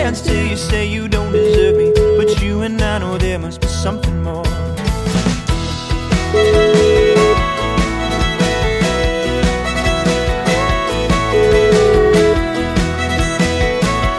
Ringback tone (Europe, 425 Hz), with noise.